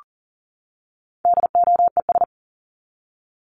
Morse code, heard as a keyed tone, about 40 wpm, 700 Hz. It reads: DOES
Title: ES:S Suffix - Top 100 Words with Base Words 40wpm